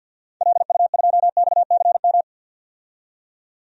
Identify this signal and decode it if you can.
Morse code, heard as a keyed tone, about 51 wpm, 700 Hz. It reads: LU2XCM
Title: DXCC-100 Call Signs - 6 Characters - 50wpm